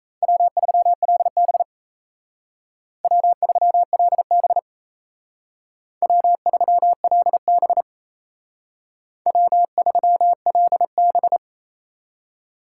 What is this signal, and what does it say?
Morse code, heard as a keyed tone, about 33 wpm, 700 Hz. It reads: W3LB W3LB W3LB W3LB
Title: Speed Racing - USA 1x2 Call Signs 28wpm